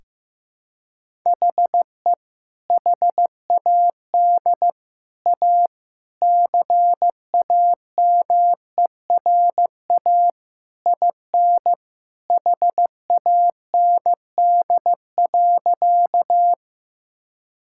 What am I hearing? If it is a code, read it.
Morse code, heard as a keyed tone, about 15 wpm, 700 Hz. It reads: HE HAD A CAMERA IN HAND.
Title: Sentences from Top 1000 Words - Review Part 5 - 15wpm